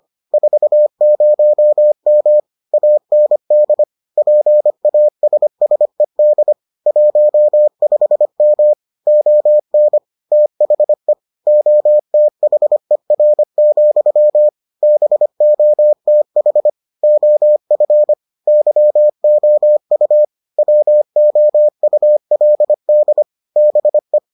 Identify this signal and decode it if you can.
Morse code, heard as a keyed tone, about 25 wpm, 600 Hz. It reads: 40M AND PASSED 15M ON THE OTHER, BOTH OF YOU WOULD BE